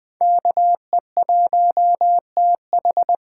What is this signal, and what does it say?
Morse code, heard as a keyed tone, about 21 wpm, 700 Hz. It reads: KE1TH